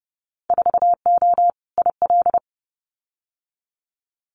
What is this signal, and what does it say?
Morse code, heard as a keyed tone, about 30 wpm, 700 Hz. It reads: 4O IL
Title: Winter Field Day Exchanges 30wpm